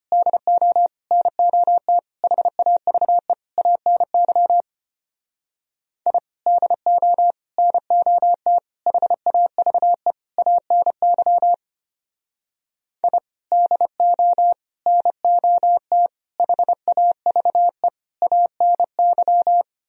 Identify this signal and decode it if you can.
Morse code, heard as a keyed tone, about 32 wpm, 700 Hz. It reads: DO NOT HAVE ANY I DO NOT HAVE ANY I DO NOT HAVE ANY